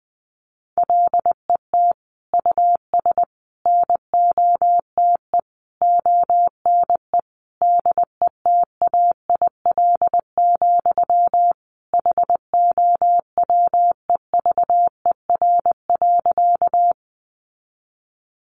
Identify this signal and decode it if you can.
Morse code, heard as a keyed tone, about 20 wpm, 700 Hz. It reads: LET US NOTE ONE DETAIL, HOWEVER.
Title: Sentences from Top 1500 Words 20wpm